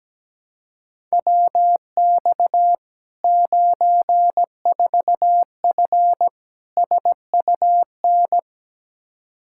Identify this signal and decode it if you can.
Morse code, heard as a keyed tone, about 17 wpm, 700 Hz. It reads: WX 94F SUN
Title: Practice QSO - Exchange 2 - Info - 17wpm